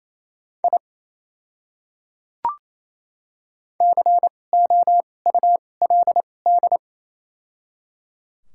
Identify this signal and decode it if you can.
Morse code, heard as a keyed tone, about 28 wpm, spaced as for 20 wpm, 700 Hz. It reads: I COULD
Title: 2x Character Spacing - Top 100 Words - 28wpm